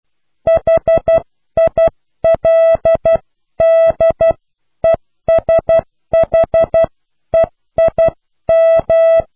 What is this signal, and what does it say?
Morse code, heard as a keyed tone, about 12 wpm, 660 Hz. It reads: HILDESHEIM
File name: cw / pseudo worden(10)